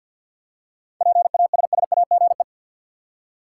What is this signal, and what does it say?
Morse code, heard as a keyed tone, about 47 wpm, 700 Hz. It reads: PASSAGE
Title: Seven Letter Words - 50wpm